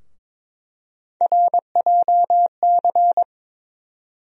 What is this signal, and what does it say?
Morse code, heard as a keyed tone, about 22 wpm, 700 Hz. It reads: RJC
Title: USA Call Sign Postfixes - 3 Letters 22wpm